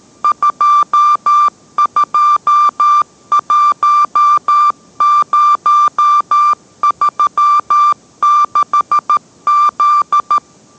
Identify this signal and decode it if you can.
Morse code, heard as a keyed tone, about 13 wpm, 1180 Hz. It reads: 221036Z